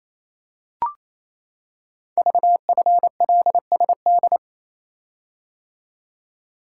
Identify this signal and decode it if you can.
Morse code, heard as a keyed tone, about 27 wpm, 700 Hz. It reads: VFLSD
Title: Code Groups - 5 Letters - 28wpm